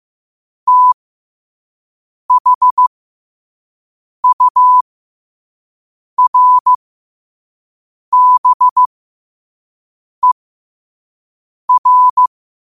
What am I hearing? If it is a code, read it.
Morse code, heard as a keyed tone, about 15 wpm, spaced as for 5 wpm, 1000 Hz. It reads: THURBER